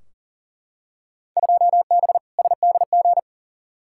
Morse code, heard as a keyed tone, about 37 wpm, 700 Hz. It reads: 2B SDG
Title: Field Day Exchanges Encore - 40wpm